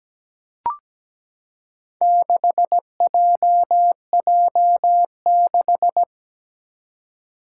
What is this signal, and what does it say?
Morse code, heard as a keyed tone, about 17 wpm, 700 Hz. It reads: 6JJ6
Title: Four Challenging Letter-Number Combinations 17wpm